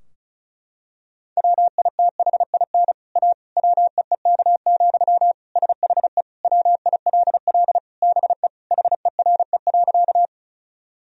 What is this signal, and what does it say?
Morse code, heard as a keyed tone, about 36 wpm, 700 Hz. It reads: WITHIN A WEEK, SHE WILL BE HERE.
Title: Sentences from Top 1000 Words - Review Part 4 - 35wpm